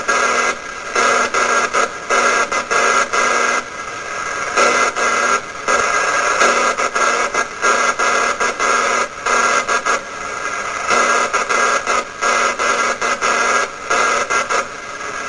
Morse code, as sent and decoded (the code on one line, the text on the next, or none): none